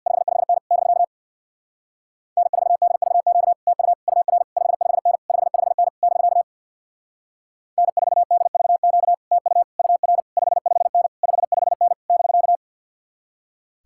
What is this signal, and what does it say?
Morse code, heard as a keyed tone, about 63 wpm, 700 Hz. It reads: 55N <BK> N4DVQ TU UR 55N 55N <BK> N4DVQ TU UR 55N 55N <BK>